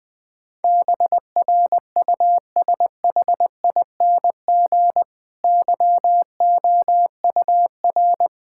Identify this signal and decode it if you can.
Morse code, heard as a keyed tone, about 20 wpm, 700 Hz. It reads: BRUSHING YOUR
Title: Difficult Sentences with Hints - 20wpm